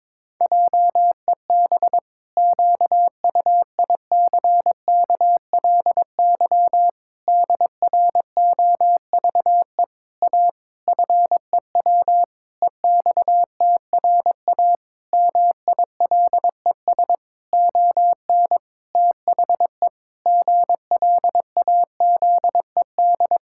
Morse code, heard as a keyed tone, about 22 wpm, 700 Hz. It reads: JEB QUICKLY DROVE A FEW EXTRA MILES ON THE GLAZED
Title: Pangram Challenge v2 - 22wpm